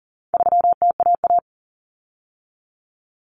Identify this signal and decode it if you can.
Morse code, heard as a keyed tone, about 39 wpm, 700 Hz. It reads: 3TAA
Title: Common Cut Numbers - 4-Digit 40wpm